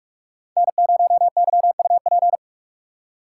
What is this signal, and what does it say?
Morse code, heard as a keyed tone, about 42 wpm, 700 Hz. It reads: N0YUP